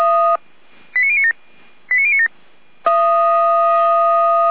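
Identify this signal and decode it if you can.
Five-tone selective call (ZVEI), heard as quick runs of five tones, tones of about 80 ms. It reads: 89097 89097
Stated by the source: ZVEI five-tone call received on SDR, channel 455.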